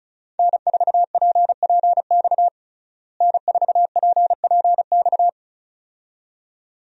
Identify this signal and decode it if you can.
Morse code, heard as a keyed tone, about 36 wpm, 700 Hz. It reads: N4PPX N4PPX
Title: USA Call Signs - 5 Characters - Repeated 2x - 35wpm